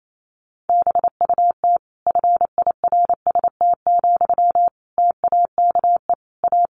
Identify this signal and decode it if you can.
Morse code, heard as a keyed tone, about 28 wpm, 700 Hz. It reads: BUT FIRST, TAKE A
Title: Sentences from Top 1400 Words 28wpm